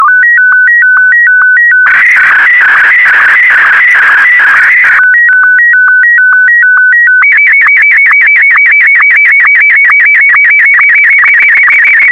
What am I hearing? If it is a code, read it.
SSTV (slow-scan television), Martin 1.
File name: sstv / M1-AX4MSL